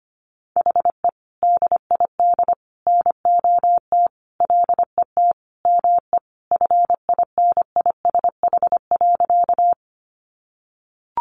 Morse code, heard as a keyed tone, about 25 wpm, 700 Hz. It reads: HE DID NOT LET ME FINISH.
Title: Sentences from Top 1200 Words 25wpm